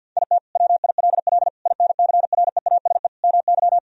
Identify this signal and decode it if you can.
Morse code, heard as a keyed tone, about 52 wpm, 700 Hz. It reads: IT WILL INCREASE MY
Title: Sentences from Top 1000 Words - Review Part 4 - 50wpm